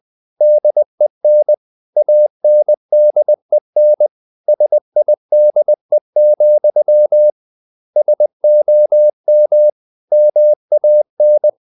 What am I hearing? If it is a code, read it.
Morse code, heard as a keyed tone, about 20 wpm, 600 Hz. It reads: DEN ANDEN SIDE, SOM MAN
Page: Ferie 20 wpm